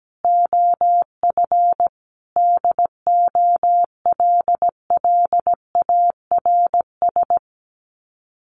Morse code, heard as a keyed tone, about 17 wpm, 700 Hz. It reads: OF DOLLARS
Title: Sets of 5 Words 17wpm